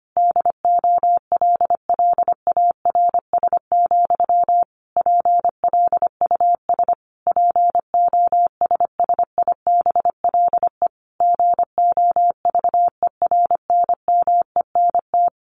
Morse code, heard as a keyed tone, about 25 wpm, 700 Hz. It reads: DOLLARS, PLUS POSSIBLE GOVERNMENT
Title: A Sound Of Thunder 25wpm